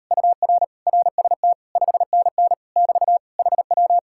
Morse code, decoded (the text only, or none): UR RST 5NN <BT> HW